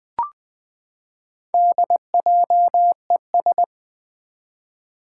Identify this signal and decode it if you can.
Morse code, heard as a keyed tone, about 20 wpm, 700 Hz. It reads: DJES
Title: Four Challenging Letter-Number Combinations 20wpm